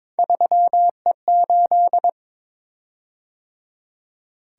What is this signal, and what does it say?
Morse code, heard as a keyed tone, about 22 wpm, 700 Hz. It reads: 3E8
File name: practice / 22wpm / Three Letter-Number Combinations - 22wpm